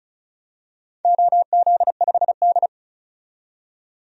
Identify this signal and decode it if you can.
Morse code, heard as a keyed tone, about 36 wpm, 700 Hz. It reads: OZ5D
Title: DXCC-100 Call Signs - 4 Characters - 35wpm